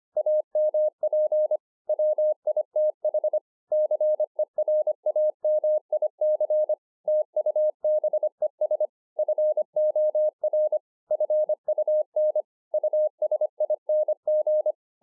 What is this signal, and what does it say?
Morse code, heard as a keyed tone, about 25 wpm, 610 Hz. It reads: AMP WITH CERAMIC TUBES FOR FUN USING